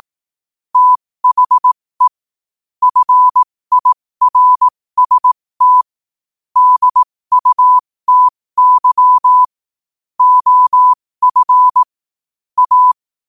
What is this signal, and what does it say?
Morse code, heard as a keyed tone, about 18 wpm, spaced as for 15 wpm, 1000 Hz. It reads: THE FIRST DUTY OF A
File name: qotdfiles / 15wpm